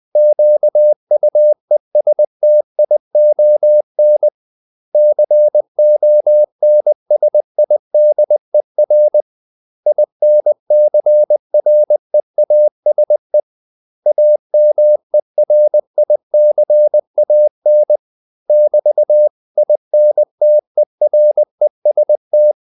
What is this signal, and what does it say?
Morse code, heard as a keyed tone, about 20 wpm, 600 Hz. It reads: QUESTION CONSIDER INCREASE AMERICAN <BT>INTEREST
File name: CW / Practice / 500MostCommonEnglishWords20